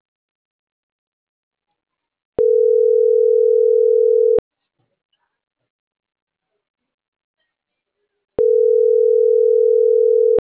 Ringback tone (North America), 440 + 480 Hz.